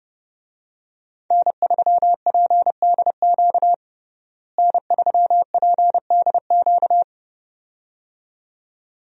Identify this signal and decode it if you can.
Morse code, heard as a keyed tone, about 30 wpm, 700 Hz. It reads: N3PDQ N3PDQ
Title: USA Call Signs - 5 Characters - Repeated 2x - 30wpm